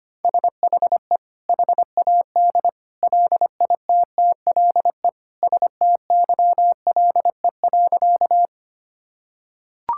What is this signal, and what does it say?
Morse code, heard as a keyed tone, about 25 wpm, 700 Hz. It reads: SHE HAD LITTLE STYLE.
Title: Sentences from Top 1000 Words - Review Part 5 - 25wpm